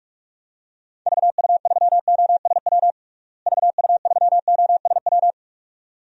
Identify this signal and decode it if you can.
Morse code, heard as a keyed tone, about 41 wpm, 700 Hz. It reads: VU3OSW VU3OSW
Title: DXCC-100 Call Signs - 6 Characters - Repeated 2x - 45wpm